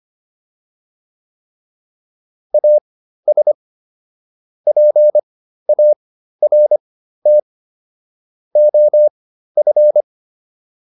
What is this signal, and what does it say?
Morse code, heard as a keyed tone, about 25 wpm, spaced as for 13 wpm, 600 Hz. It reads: AS PART OF